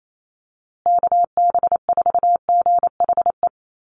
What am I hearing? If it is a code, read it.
Morse code, heard as a keyed tone, about 28 wpm, 700 Hz. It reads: KB4GHE
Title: USA Call Signs Repeated 3x 28wpm